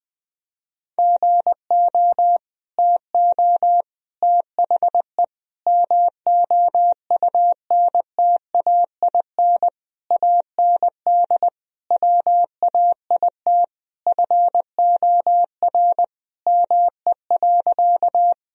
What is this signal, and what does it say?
Morse code, heard as a keyed tone, about 21 wpm, 700 Hz. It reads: GO TO THE MOUNTAIN AND WAIT FOR ME.
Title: Sentences from Top 1000 Words - Review Part 1 - 20wpm